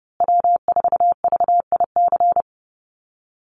Morse code, heard as a keyed tone, about 30 wpm, 700 Hz. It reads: W4VIC
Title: Call Signs Repeated 2x - 30wpm